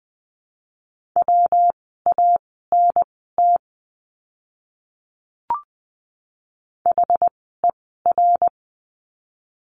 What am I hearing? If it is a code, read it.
Morse code, heard as a keyed tone, about 20 wpm, spaced as for 15 wpm, 700 Hz. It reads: WANT HER
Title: 115 Lesson 19f - Letter w - Words 20wpm